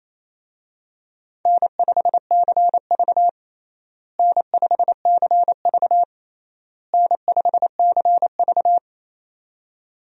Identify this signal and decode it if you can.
Morse code, heard as a keyed tone, about 28 wpm, 700 Hz. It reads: N5CV N5CV N5CV
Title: USA Call Signs - 4 Characters - Repeated 3x - 28wpm